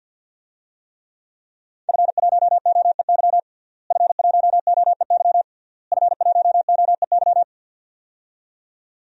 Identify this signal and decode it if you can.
Morse code, heard as a keyed tone, about 52 wpm, 700 Hz. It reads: F1OEY F1OEY F1OEY
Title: DXCC-100 Call Signs - 5 Characters - Repeated 3x - 50wpm